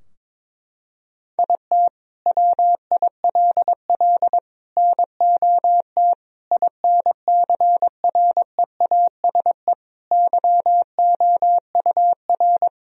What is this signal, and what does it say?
Morse code, heard as a keyed tone, about 21 wpm, 700 Hz. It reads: IT WILL NOT INCREASE YOUR
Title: Sentences from Top 1000 Words - Review Part 4 - 22wpm